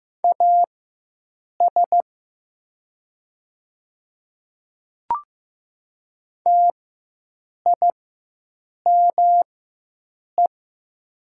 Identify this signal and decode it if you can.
Morse code, heard as a keyed tone, about 15 wpm, spaced as for 7 wpm, 700 Hz. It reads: AS TIME
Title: 4x Character Spacing - Top 100 Words - 15wpm